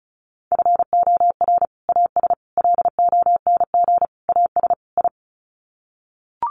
Morse code, heard as a keyed tone, about 34 wpm, 700 Hz. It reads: FOR AS LONG AS I
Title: Sets of 5 Words Encore2 35wpm